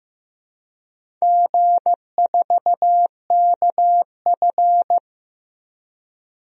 Morse code, heard as a keyed tone, about 15 wpm, 700 Hz. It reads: G4KF
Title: DXCC-100 Call Signs - 4 Characters - 15wpm